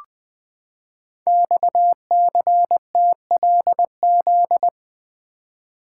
Morse code, heard as a keyed tone, about 20 wpm, 700 Hz. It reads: XCTLZ
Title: Code Groups - 5 Letters - 20wpm